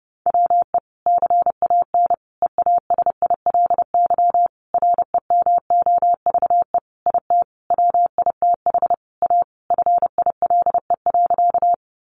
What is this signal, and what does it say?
Morse code, heard as a keyed tone, about 30 wpm, 700 Hz. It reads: WE CAN EASILY REMOVE IT WITH A FILE.
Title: Sentences from Top 1200 Words 30wpm